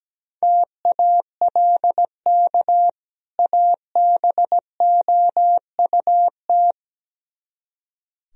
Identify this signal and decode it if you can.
Morse code, heard as a keyed tone, about 18 wpm, 700 Hz. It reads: TALK ABOUT
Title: Speed-Racing - Sets of 2 Words 17wpm